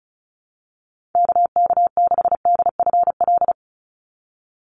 Morse code, heard as a keyed tone, about 34 wpm, 700 Hz. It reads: KK6DFL
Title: USA Call Signs 35wpm